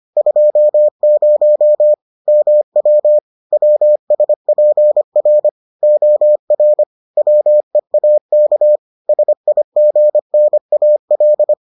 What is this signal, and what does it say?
Morse code, heard as a keyed tone, about 25 wpm, 600 Hz. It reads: 20 MW WSPR OR WEAK SIGNAL